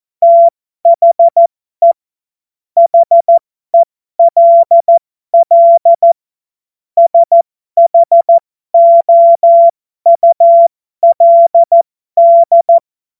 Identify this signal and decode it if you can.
Morse code, heard as a keyed tone, about 14 wpm, spaced as for 12 wpm, 680 Hz. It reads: THE HELL SHOULD